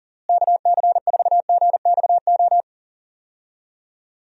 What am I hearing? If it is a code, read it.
Morse code, heard as a keyed tone, about 40 wpm, 700 Hz. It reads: KC4GXO
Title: USA Call Signs - 6 Characters - Repeated 3x - 40wpm